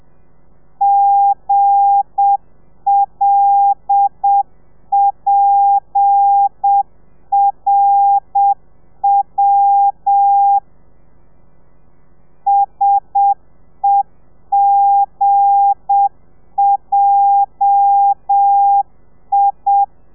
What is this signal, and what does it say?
Morse code, heard as a keyed tone, about 7 wpm, 800 Hz. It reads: GLPRW SEGJI
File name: telegrafia / i2tfj / Lez8 35